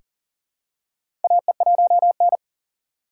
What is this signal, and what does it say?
Morse code, heard as a keyed tone, about 37 wpm, 700 Hz. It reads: AE1N
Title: K1USN SST QSO Practice - 40wpm